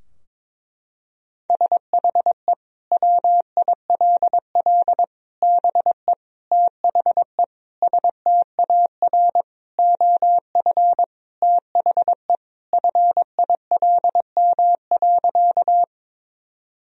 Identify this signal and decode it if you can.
Morse code, heard as a keyed tone, about 22 wpm, 700 Hz. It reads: SHE WILL BE THE STAR OF THE FILM.
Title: Sentences from Top 1000 Words - Review Part 3 - 22wpm